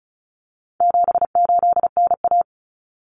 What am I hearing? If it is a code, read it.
Morse code, heard as a keyed tone, about 34 wpm, 700 Hz. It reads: 78NA
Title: Common Cut Numbers - 4-Digit 35wpm